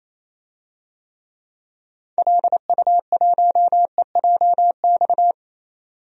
Morse code, heard as a keyed tone, about 27 wpm, 700 Hz. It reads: LU1EJX
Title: DXCC-100 Call Signs - 6 Characters - 28wpm